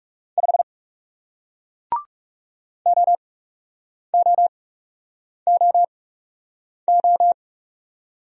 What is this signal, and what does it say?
Morse code, heard as a keyed tone, about 37 wpm, spaced as for 8 wpm, 700 Hz. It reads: 5 OOOO
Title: Speed-Racing Single Letter-Number 30wpm